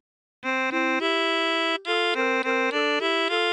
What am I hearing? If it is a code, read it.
There is no code.